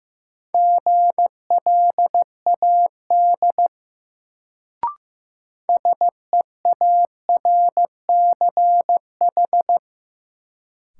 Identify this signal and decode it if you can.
Morse code, heard as a keyed tone, about 15 wpm, 700 Hz. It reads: GLAD SEARCH
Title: Sentences from Top 800 Words - New Word Drill - 15wpm